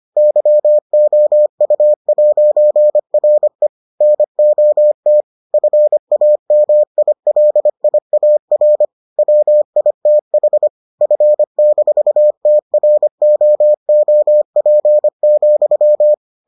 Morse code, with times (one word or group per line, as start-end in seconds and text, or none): 0.2-3.7 YOU'RE
4.0-5.2 NOT
5.5-8.8 FAMILIAR
9.2-10.7 WITH
11.0-16.1 F-TROOP,